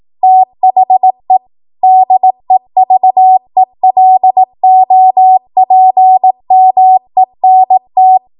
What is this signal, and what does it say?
Morse code, heard as a keyed tone, about 18 wpm, 750 Hz. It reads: THE DEVELOPMENT
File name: Morse / 18WPM